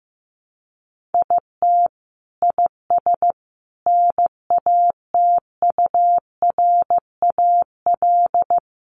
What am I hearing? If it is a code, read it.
Morse code, heard as a keyed tone, about 15 wpm, 700 Hz. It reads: IT IS NATURAL
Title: Sentences from Top 700 Words 15wpm